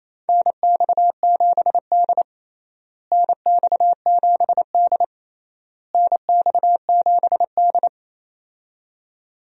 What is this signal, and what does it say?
Morse code, heard as a keyed tone, about 27 wpm, 700 Hz. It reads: NX7D NX7D NX7D
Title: USA Call Signs - 4 Characters - Repeated 3x - 28wpm